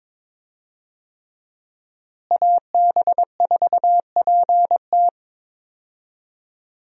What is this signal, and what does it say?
Morse code, heard as a keyed tone, about 22 wpm, 700 Hz. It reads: AB4PT
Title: USA Call Signs - 5 Characters - Repeated 3x - 22wpm